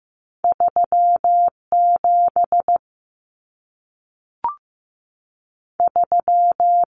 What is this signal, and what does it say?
Morse code, heard as a keyed tone, about 15 wpm, 700 Hz. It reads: 37 3
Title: Common Cut Numbers - 2-Digit 15wpm